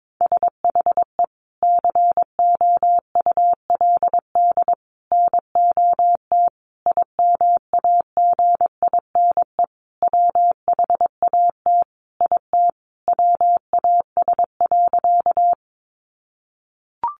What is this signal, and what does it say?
Morse code, heard as a keyed tone, about 22 wpm, 700 Hz. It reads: SHE COULD NOT IMAGINE WHAT IT WAS.
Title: Sentences from Top 1400 Words 22wpm